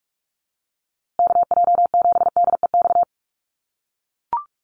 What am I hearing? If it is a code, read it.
Morse code, heard as a keyed tone, about 44 wpm, 700 Hz. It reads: KJ7DEX
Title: USA Call Signs Repeated 3x 45wpm